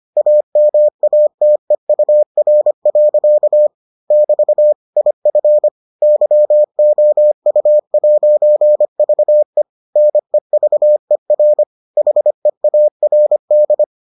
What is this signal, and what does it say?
Morse code, heard as a keyed tone, about 25 wpm, 600 Hz. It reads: AMATEUR. <BT> IF YOU'VE NEVER HEARD